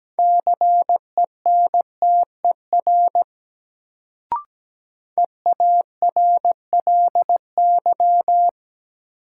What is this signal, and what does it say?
Morse code, heard as a keyed tone, about 17 wpm, 700 Hz. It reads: CENTER EARLY
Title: Sets of 3 Words - New Word Drill - 17wpm